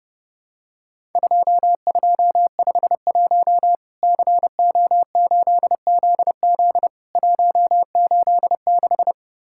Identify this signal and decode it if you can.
Morse code, heard as a keyed tone, about 31 wpm, 700 Hz. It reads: 2251 CO8ZZ 186
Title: Worked All Europe DX Contest - QTC Exchange - 30wpm